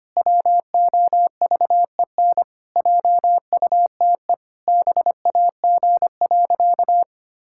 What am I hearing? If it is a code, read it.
Morse code, heard as a keyed tone, about 25 wpm, 700 Hz. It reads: WOVEN JUTE BAG.
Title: Pangram Challenge v2 - 25wpm